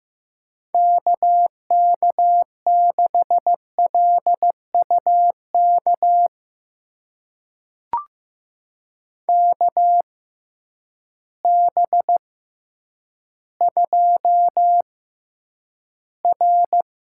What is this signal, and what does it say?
Morse code, heard as a keyed tone, about 16 wpm, spaced as for 9 wpm, 700 Hz. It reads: KK6LUK KB2R
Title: USA Call Signs - 6 Characters - 6x Character Spacing - 15wpm